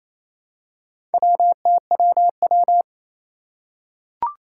Morse code, heard as a keyed tone, about 27 wpm, 700 Hz. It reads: WTWW
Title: Four Challenging Letter-Number Combinations 28wpm